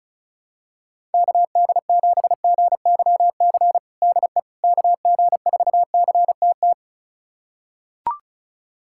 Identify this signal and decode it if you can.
Morse code, heard as a keyed tone, about 36 wpm, 700 Hz. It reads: KD7GYC DE KG4CTT
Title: USA Callsign DE Callsign - 6 Characters - 35wpm